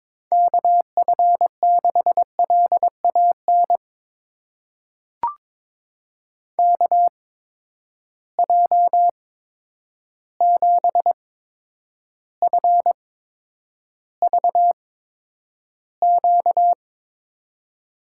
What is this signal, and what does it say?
Morse code, heard as a keyed tone, about 22 wpm, spaced as for 9 wpm, 700 Hz. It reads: KF6LAN KJ7FVQ
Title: USA Call Signs - 6 Characters - 8x Character Spacing - 22wpm